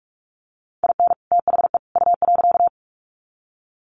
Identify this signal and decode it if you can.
Morse code, heard as a keyed tone, about 44 wpm, 700 Hz. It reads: IN THE U.
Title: Sets of 3 Words 45wpm